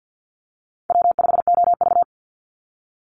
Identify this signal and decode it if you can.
Morse code, heard as a keyed tone, about 50 wpm, 700 Hz. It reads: W5OV
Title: Call Signs - 50wpm